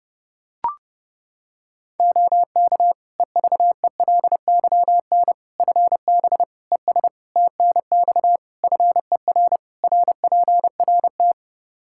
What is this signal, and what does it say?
Morse code, heard as a keyed tone, about 32 wpm, 700 Hz. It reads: OK EVELYN FB ES TNX FER RPRT
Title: Practice QSO - Exchange 2 - Info - 30wpm